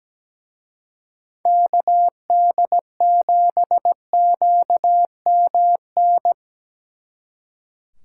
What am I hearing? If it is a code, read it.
Morse code, heard as a keyed tone, about 20 wpm, 700 Hz. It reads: KD7QMN
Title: Speed Racing - USA 2x3 Call Signs 17wpm